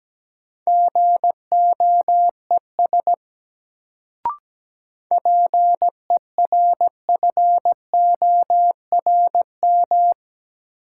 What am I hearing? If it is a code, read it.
Morse code, heard as a keyed tone, about 17 wpm, 700 Hz. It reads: GOES PERFORM
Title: Sentences from Top 1500 Words - New Word Drill Alt - 17wpm